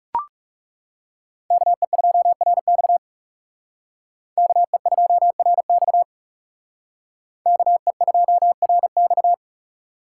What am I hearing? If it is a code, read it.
Morse code, heard as a keyed tone, about 38 wpm, 700 Hz. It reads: KE2RX KE2RX KE2RX